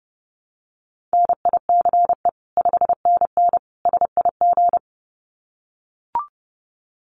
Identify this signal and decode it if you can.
Morse code, heard as a keyed tone, about 30 wpm, 700 Hz. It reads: NICE 5NN SIG
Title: Common QSO Phrases 30wpm